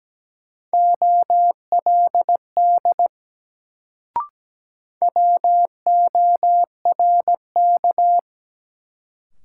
Morse code, heard as a keyed tone, about 17 wpm, 700 Hz. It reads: OLD WORK
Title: Sentences from Top 200 Words - New Word Drill - 17wpm